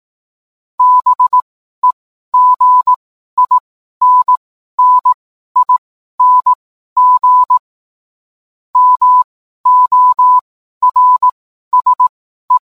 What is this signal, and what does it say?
Morse code, heard as a keyed tone, about 18 wpm, spaced as for 12 wpm, 1000 Hz. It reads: BEGINNING MORSE